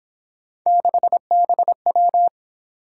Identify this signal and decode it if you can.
Morse code, heard as a keyed tone, about 26 wpm, 700 Hz. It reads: 6BW
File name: Speed-Racing - Three Letter-Number Combinations - 22wpm